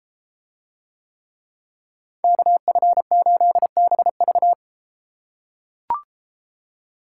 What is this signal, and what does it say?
Morse code, heard as a keyed tone, about 31 wpm, 700 Hz. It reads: KF8BV